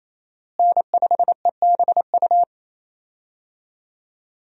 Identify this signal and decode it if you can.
Morse code, heard as a keyed tone, about 27 wpm, 700 Hz. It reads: N5EBU
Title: USA Call Signs - 5 Characters - Repeated 3x - 28wpm